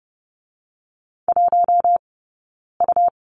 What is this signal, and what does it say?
Morse code, heard as a keyed tone, about 30 wpm, 700 Hz. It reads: 1 U
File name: course / 30:10wpm / 099 Lesson 17d - Letter u - Two Characters 30wpm